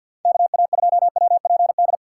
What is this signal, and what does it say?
Morse code, heard as a keyed tone, about 52 wpm, 700 Hz. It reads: KA2WPD
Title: USA Call Signs - 6x Character Spacing - 50wpm